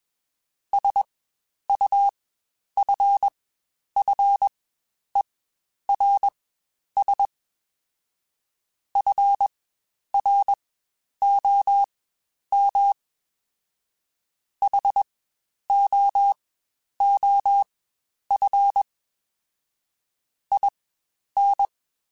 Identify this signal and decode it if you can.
Morse code, heard as a keyed tone, about 21 wpm, spaced as for 10 wpm, 800 Hz. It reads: SUFFERS FROM HOOF IN
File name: Quotes Spaced / Quotes Spaced-mp3s / HoofInMouth sp @20